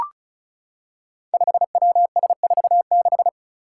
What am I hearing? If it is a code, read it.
Morse code, heard as a keyed tone, about 36 wpm, 700 Hz. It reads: 5WS46
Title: Five Challenging Letter-Number Combinations 35wpm